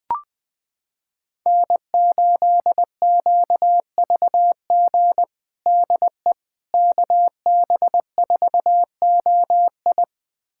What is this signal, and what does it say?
Morse code, heard as a keyed tone, about 20 wpm, 700 Hz. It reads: N8QVG DE KB4OI
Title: USA Callsign DE Callsign - 5 Characters - 20wpm